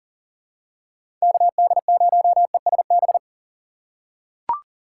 Morse code, heard as a keyed tone, about 40 wpm, 700 Hz. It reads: KD0ESB
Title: USA Call Signs - 6 Characters - 6x Character Spacing - 40wpm